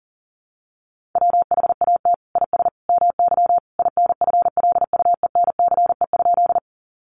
Morse code, heard as a keyed tone, about 40 wpm, 700 Hz. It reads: WHAT IS MY INFLUENCE?
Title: Sentences from Top 1100 Words 40wpm